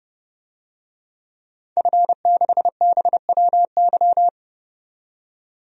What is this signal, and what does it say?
Morse code, heard as a keyed tone, about 30 wpm, 700 Hz. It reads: F6BWY
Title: DXCC-100 Call Signs - 5 Characters - 30wpm